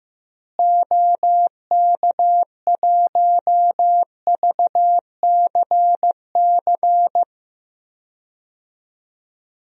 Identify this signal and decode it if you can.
Morse code, heard as a keyed tone, about 15 wpm, 700 Hz. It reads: OK1VCC
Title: DXCC-100 Call Signs - 6 Characters - Repeated 2x - 15wpm